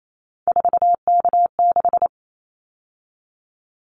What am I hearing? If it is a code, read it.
Morse code, heard as a keyed tone, about 28 wpm, 700 Hz. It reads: 4K6
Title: Callsign Prefixes for Top 100 DXCC - 28wpm